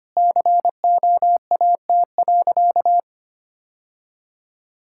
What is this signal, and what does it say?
Morse code, heard as a keyed tone, about 25 wpm, 700 Hz. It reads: COAT.
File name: practice / 25wpm / Sentences with 2-Word Phrases - 25wpm